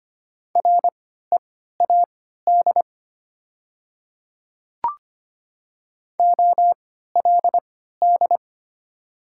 Morse code, heard as a keyed tone, about 25 wpm, spaced as for 14 wpm, 690 Hz. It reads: READ OLD
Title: 3x Character Spacing - Top 200 Words - 25wpm